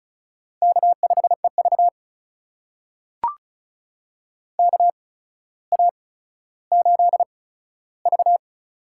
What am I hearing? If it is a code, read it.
Morse code, heard as a keyed tone, about 36 wpm, spaced as for 15 wpm, 700 Hz. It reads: K5EV KA8V
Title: USA Call Signs - 8x Character Spacing - 35wpm